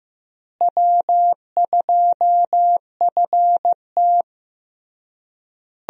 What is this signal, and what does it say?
Morse code, heard as a keyed tone, about 16 wpm, 700 Hz. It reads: W2FT
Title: USA Call Signs - 4 Characters - Repeated 3x - 15wpm